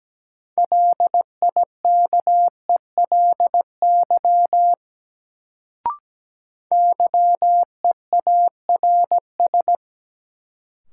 Morse code, heard as a keyed tone, about 17 wpm, 700 Hz. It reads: LIKELY YEARS
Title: Sets of 2 Words - New Word Drill - 17wpm